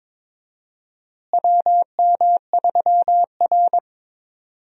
Morse code, heard as a keyed tone, about 22 wpm, 700 Hz. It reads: WM3R
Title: USA Call Signs - 4 Characters - Repeated 3x - 22wpm